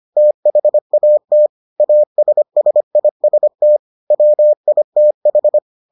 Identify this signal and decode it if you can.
Morse code, heard as a keyed tone, about 25 wpm, 600 Hz. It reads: THAT ASSIST WITH